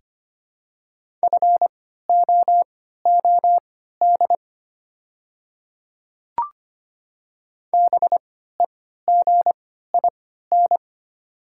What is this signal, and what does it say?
Morse code, heard as a keyed tone, about 25 wpm, spaced as for 14 wpm, 700 Hz. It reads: FOOD BEGIN
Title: 3x Character Spacing - Top 300 Words - 25wpm